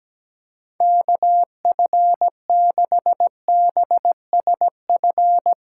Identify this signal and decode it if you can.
Morse code, heard as a keyed tone, about 17 wpm, 700 Hz. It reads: KF6BSF
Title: USA Call Signs - 6 Characters - 4x Character Spacing - 17wpm